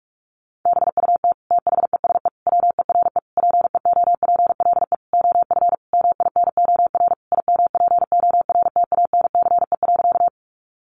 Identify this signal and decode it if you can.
Morse code, heard as a keyed tone, about 44 wpm, 700 Hz. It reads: BUT THESE WERE PEOPLE OF MINOR IMPORTANCE.
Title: Sentences from Top 1500 Words 45wpm